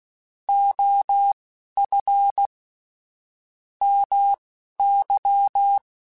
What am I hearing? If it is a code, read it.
Morse code, heard as a keyed tone, about 16 wpm, spaced as for 11 wpm, 800 Hz. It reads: OF MY
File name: QSOs / QSOs-mp3s / WD0FBT 1@10